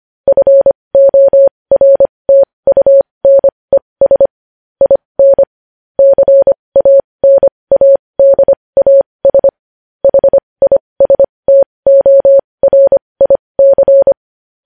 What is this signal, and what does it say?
Morse code, heard as a keyed tone, about 25 wpm, spaced as for 20 wpm, 550 Hz. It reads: FORTUNES IN CANADAS HISTORIC
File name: podcast / News headlines-1x-25x20